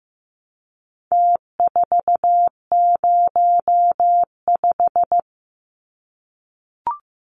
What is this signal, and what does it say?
Morse code, heard as a keyed tone, about 15 wpm, 700 Hz. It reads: T405
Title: Numbers - Contest Style - 4-Digit 15wpm